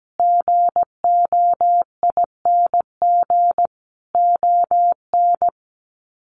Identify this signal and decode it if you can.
Morse code, heard as a keyed tone, about 17 wpm, 700 Hz. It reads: GOING ON
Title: Sets of 5 Words Encore2 17wpm